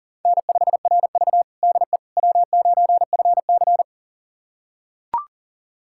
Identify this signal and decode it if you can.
Morse code, heard as a keyed tone, about 40 wpm, 700 Hz. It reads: N5RV DE W9FC
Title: USA Callsign DE Callsign - 4 Characters - 40wpm